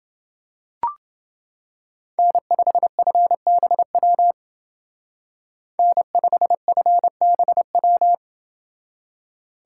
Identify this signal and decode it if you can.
Morse code, heard as a keyed tone, about 28 wpm, 700 Hz. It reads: N5FBW N5FBW